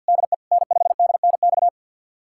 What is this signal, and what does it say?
Morse code, heard as a keyed tone, about 47 wpm, 700 Hz. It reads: DE N5DNX
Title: USA Callsign DE Callsign - 5 Characters - 50wpm